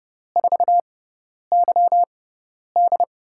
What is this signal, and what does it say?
Morse code, heard as a keyed tone, about 31 wpm, spaced as for 10 wpm, 700 Hz. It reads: 4YD
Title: USA Call Signs - 6 Characters - 6x Character Spacing - 30wpm